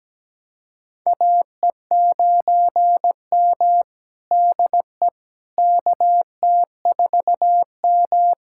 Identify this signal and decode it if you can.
Morse code, heard as a keyed tone, about 17 wpm, 700 Hz. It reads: AE9M DE KT4M
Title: USA Callsign DE Callsign - 4 Characters - 17wpm